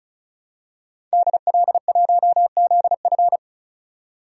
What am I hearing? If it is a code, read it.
Morse code, heard as a keyed tone, about 36 wpm, 700 Hz. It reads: DL1ZF
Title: DXCC-100 Call Signs - 5 Characters - 35wpm